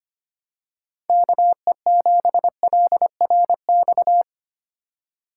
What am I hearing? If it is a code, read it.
Morse code, heard as a keyed tone, about 28 wpm, 700 Hz. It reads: KE7LRX
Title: Speed Racing - USA 2x3 Call Signs 25wpm